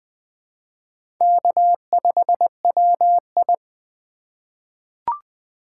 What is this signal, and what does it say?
Morse code, heard as a keyed tone, about 20 wpm, 700 Hz. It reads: K5WI
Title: 2023 SOTA Chaser Call Signs - 20wpm